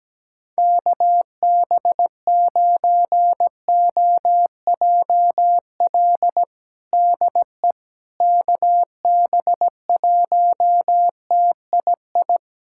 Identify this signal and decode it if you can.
Morse code, heard as a keyed tone, about 17 wpm, 700 Hz. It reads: KB9OJL DE KB1TII
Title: USA Callsign DE Callsign 17wpm